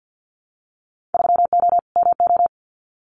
Morse code, heard as a keyed tone, about 50 wpm, 700 Hz. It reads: 3O MO
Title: Winter Field Day Exchanges Encore 50wpm